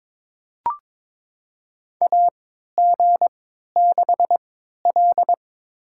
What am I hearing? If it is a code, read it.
Morse code, heard as a keyed tone, about 23 wpm, spaced as for 13 wpm, 700 Hz. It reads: AG6L
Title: USA Call Signs - 4 Characters - 3x Character Spacing - 22wpm